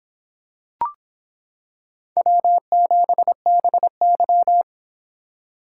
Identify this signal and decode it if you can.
Morse code, heard as a keyed tone, about 25 wpm, 700 Hz. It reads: W7BY